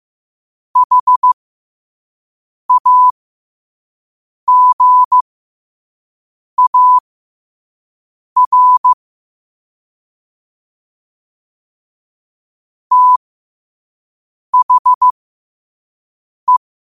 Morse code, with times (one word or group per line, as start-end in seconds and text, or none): 0.8-8.9 HAGAR
12.9-16.6 THE